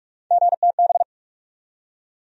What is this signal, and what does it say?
Morse code, heard as a keyed tone, about 46 wpm, 700 Hz. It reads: GTB
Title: USA Call Sign Postfixes - 3 Letters 45wpm